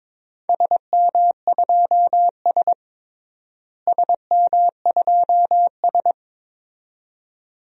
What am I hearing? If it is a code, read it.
Morse code, heard as a keyed tone, about 22 wpm, 700 Hz. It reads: SM2S SM2S
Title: DXCC-100 Call Signs - 4 Characters - Repeated 2x - 22wpm